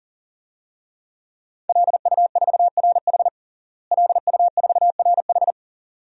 Morse code, heard as a keyed tone, about 40 wpm, 700 Hz. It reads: LU4RH LU4RH
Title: DXCC-100 Call Signs - 5 Characters - Repeated 2x - 40wpm